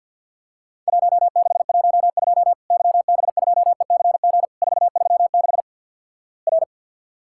Morse code, heard as a keyed tone, about 50 wpm, 700 Hz. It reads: 1612 YB2ECG 436 R
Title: Worked All Europe DX Contest - QTC Exchange - 50wpm